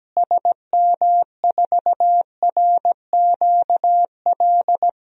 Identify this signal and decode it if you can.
Morse code, heard as a keyed tone, about 17 wpm, 700 Hz. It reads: SM4RQL
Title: DXCC-100 Call Signs - 6 Characters - Repeated 2x - 17wpm